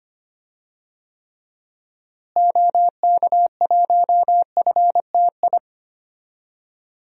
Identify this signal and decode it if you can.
Morse code, heard as a keyed tone, about 25 wpm, 700 Hz. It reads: OK1FTI
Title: DXCC-100 Call Signs - 6 Characters - Repeated 2x - 25wpm